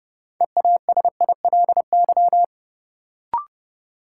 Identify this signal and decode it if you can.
Morse code, heard as a keyed tone, about 31 wpm, 700 Hz. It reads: EASILY
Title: Sentences from Top 900 Words - New Word Drill - 30wpm